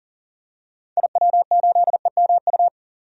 Morse code, heard as a keyed tone, about 40 wpm, 700 Hz. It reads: IW8EMU
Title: DXCC-100 Call Signs - 6 Characters - 40wpm